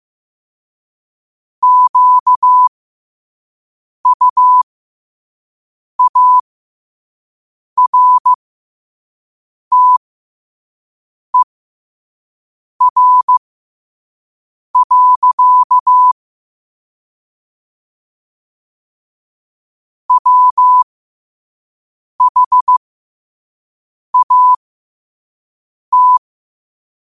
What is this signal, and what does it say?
Morse code, heard as a keyed tone, about 15 wpm, spaced as for 5 wpm, 1000 Hz. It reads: QUARTER. WHAT
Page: New quotes every day in morse code at 5 Words per minute.